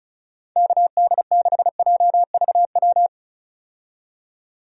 Morse code, heard as a keyed tone, about 36 wpm, 700 Hz. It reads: KD6JVW
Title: USA Call Signs - 6 Characters - 35wpm